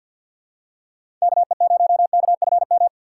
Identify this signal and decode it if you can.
Morse code, heard as a keyed tone, about 47 wpm, 700 Hz. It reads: KE0KFM
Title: USA Call Signs - 4x Character Spacing - 50wpm